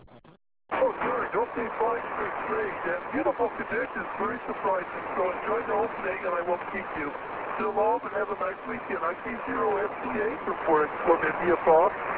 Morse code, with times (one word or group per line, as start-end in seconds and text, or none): none